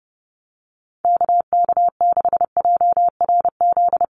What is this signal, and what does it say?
Morse code, heard as a keyed tone, about 30 wpm, 700 Hz. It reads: KK6JRZ
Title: USA Call Signs Repeated 2x 30wpm